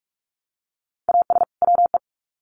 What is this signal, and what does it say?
Morse code, heard as a keyed tone, about 44 wpm, 700 Hz. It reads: AS WE
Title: Sets of 2 Words 45wpm